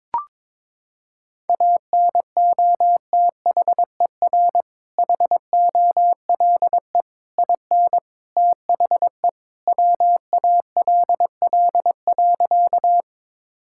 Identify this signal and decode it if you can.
Morse code, heard as a keyed tone, about 22 wpm, 700 Hz. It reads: ANOTHER HOLE IN THE WALL.
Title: Sentences from Top 1600 Words 22wpm